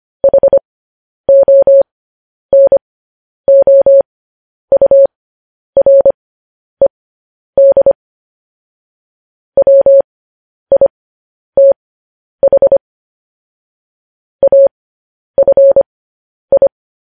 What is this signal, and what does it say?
Morse code, heard as a keyed tone, about 25 wpm, spaced as for 10 wpm, 550 Hz. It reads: HONOURED WITH AFI